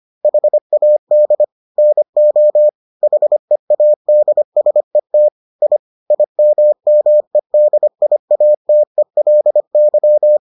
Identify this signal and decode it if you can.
Morse code, heard as a keyed tone, about 25 wpm, 600 Hz. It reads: HAD NO HEADSET I IMMEDIATELY